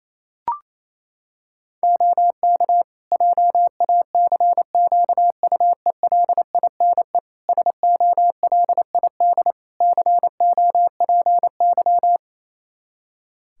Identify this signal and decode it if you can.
Morse code, heard as a keyed tone, about 27 wpm, 700 Hz. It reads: OK JACQUELINE SOLID COPY
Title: K1USN SST QSO Practice 28wpm